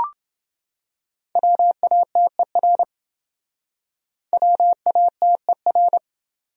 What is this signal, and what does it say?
Morse code, heard as a keyed tone, about 28 wpm, 700 Hz. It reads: WATER WATER